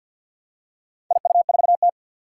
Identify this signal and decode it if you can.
Morse code, heard as a keyed tone, about 50 wpm, 700 Hz. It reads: IU4T
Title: Four Challenging Letter-Number Combinations 50wpm